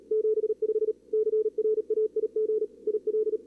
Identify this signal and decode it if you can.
Morse code, heard as a keyed tone, about 34 wpm, 440 Hz. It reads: 75 CRAIG IL
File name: CWSprint / Feb2022 / k5tr-feb-na-sprint-cw-k5ot -0000